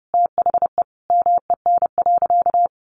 Morse code, heard as a keyed tone, about 30 wpm, 700 Hz. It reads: THE MEN.
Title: Sentences from Top 1400 Words 30wpm